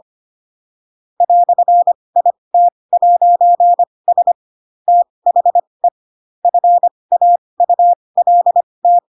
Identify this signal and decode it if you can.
Morse code, heard as a keyed tone, about 25 wpm, spaced as for 20 wpm, 700 Hz. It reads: "IT'S THE FAULT
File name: Pinocchio / Pinocchio ch2 15 25 ik1hge